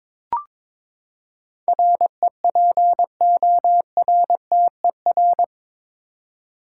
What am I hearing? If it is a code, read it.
Morse code, heard as a keyed tone, about 22 wpm, 700 Hz. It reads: REPORTER
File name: practice / 22wpm / Eight Letter Words - 22wpm